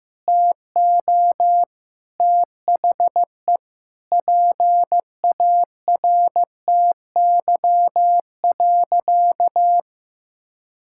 Morse code, heard as a keyed tone, about 15 wpm, 700 Hz. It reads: TO THE PARTY.
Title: Sentences from Top 1600 Words 15wpm